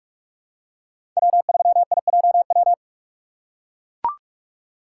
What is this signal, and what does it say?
Morse code, heard as a keyed tone, about 41 wpm, 700 Hz. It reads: W3IJW
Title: USA Call Signs - 5 Characters - 6x Character Spacing - 45wpm